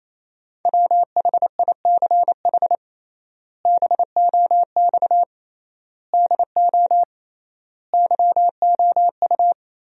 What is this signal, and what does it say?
Morse code, heard as a keyed tone, about 28 wpm, 700 Hz. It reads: WHICH BOX DO YOU
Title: 3x Word-Spacing - Sentences from Top 500 Words - 28wpm